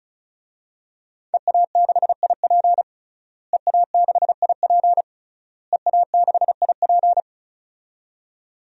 Morse code, heard as a keyed tone, about 34 wpm, 700 Hz. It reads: EA6IP EA6IP EA6IP
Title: DXCC-100 Call Signs - 5 Characters - Repeated 3x - 35wpm